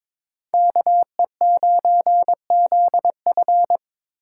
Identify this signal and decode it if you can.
Morse code, heard as a keyed tone, about 26 wpm, 700 Hz. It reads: KE9ZF
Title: Speed Racing - USA 2x2 Call Signs 22wpm